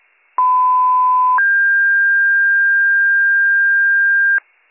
Two-tone page, heard as a one-second tone, then a three-second tone, about 1010 then 1600 Hz. It is heard from 0.4 to 4.4 s.